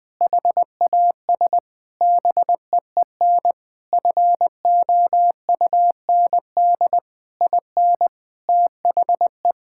Morse code, heard as a keyed tone, about 20 wpm, 700 Hz. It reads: HAS BEEN FOUND IN THE
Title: Sentences from Top 1000 Words - Review Part 5 - 20wpm